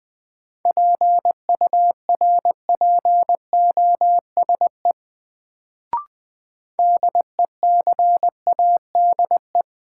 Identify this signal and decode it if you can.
Morse code, heard as a keyed tone, about 20 wpm, 700 Hz. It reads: PURPOSE DECADE
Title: Sets of 5 Words - New Word Drill - 20wpm